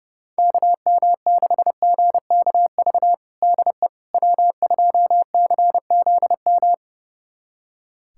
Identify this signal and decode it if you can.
Morse code, heard as a keyed tone, about 32 wpm, 700 Hz. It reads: KM6GKV DE W2CZM
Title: USA Callsign DE Callsign 30wpm